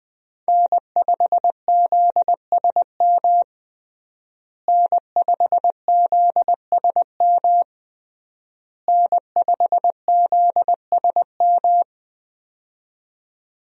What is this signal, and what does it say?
Morse code, heard as a keyed tone, about 21 wpm, 700 Hz. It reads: N5ZSM N5ZSM N5ZSM
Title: USA Call Signs - 5 Characters - Repeated 3x - 20wpm